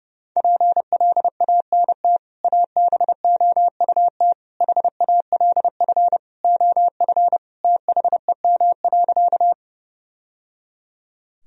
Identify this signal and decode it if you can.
Morse code, heard as a keyed tone, about 32 wpm, 700 Hz. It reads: PLANT ABOUT HALF OF THEM.
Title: Sentences from Top 1000 Words - Review Part 1 - 30wpm